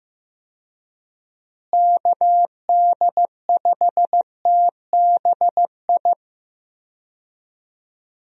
Morse code, heard as a keyed tone, about 15 wpm, 700 Hz. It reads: KD5TBI
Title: USA Call Signs - 6 Characters - Repeated 2x - 15wpm